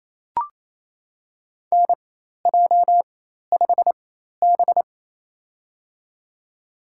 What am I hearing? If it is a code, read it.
Morse code, heard as a keyed tone, about 28 wpm, spaced as for 13 wpm, 700 Hz. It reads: NJ5B
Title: USA Call Signs - 4 Characters - 4x Character Spacing - 28wpm